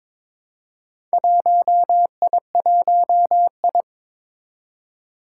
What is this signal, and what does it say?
Morse code, heard as a keyed tone, about 23 wpm, 700 Hz. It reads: 1I1I
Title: Four Challenging Letter-Number Combinations 22wpm